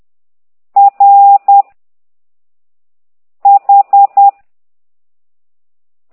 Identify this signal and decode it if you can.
Morse code, heard as a keyed tone, about 10 wpm, 800 Hz. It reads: R H
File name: ct2idw / exerc9